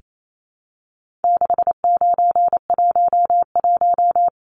Morse code, heard as a keyed tone, about 28 wpm, 700 Hz. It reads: 6911
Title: Numbers - 4-Digit 28wpm